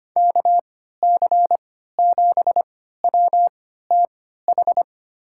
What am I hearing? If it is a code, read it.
Morse code, heard as a keyed tone, about 25 wpm, spaced as for 14 wpm, 700 Hz. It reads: KC7WTH
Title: USA Call Signs - 3x Character Spacing - 25wpm